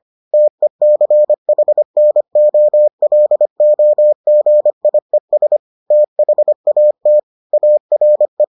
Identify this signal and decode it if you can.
Morse code, heard as a keyed tone, about 25 wpm, 600 Hz. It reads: TECHNOLOGIES THAT ARE